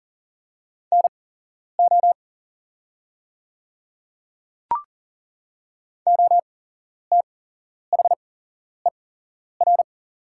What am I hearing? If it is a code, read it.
Morse code, heard as a keyed tone, about 39 wpm, spaced as for 11 wpm, 700 Hz. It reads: NO OTHER
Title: 8x Character Spacing - Top 200 Words - 40wpm